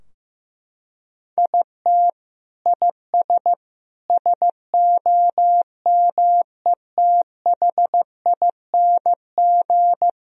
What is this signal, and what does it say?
Morse code, heard as a keyed tone, about 16 wpm, 700 Hz. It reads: IT IS SOMETHING
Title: Sentences from Top 1000 Words - Review Part 3 - 15wpm